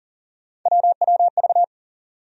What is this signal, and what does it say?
Morse code, heard as a keyed tone, about 40 wpm, 700 Hz. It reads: WWV
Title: USA Call Sign Postfixes - 3 Letters 40wpm